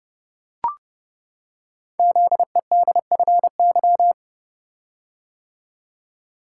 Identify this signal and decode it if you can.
Morse code, heard as a keyed tone, about 30 wpm, 700 Hz. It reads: ZEDFY
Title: Code Groups - 5 Letters - 30wpm